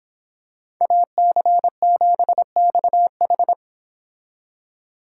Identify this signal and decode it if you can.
Morse code, heard as a keyed tone, about 25 wpm, 700 Hz. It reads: AC7XH